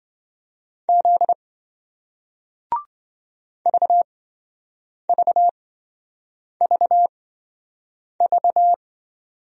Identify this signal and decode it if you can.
Morse code, heard as a keyed tone, about 25 wpm, spaced as for 7 wpm, 700 Hz. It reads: Z VVVV
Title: Speed-Racing Single Letters 20wpm